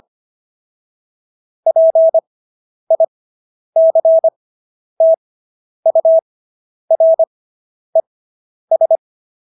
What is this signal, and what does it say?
Morse code, heard as a keyed tone, about 25 wpm, spaced as for 10 wpm, 650 Hz. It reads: PICTURES